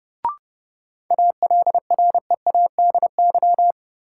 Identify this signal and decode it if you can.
Morse code, heard as a keyed tone, about 30 wpm, 700 Hz. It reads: ALREADY
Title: Sentences from Top 700 Words - New Word Drill - 30wpm